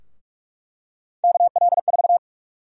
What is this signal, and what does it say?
Morse code, heard as a keyed tone, about 46 wpm, 700 Hz. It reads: KL4
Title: USA Call Sign Prefixes - Repeated 2x - 45wpm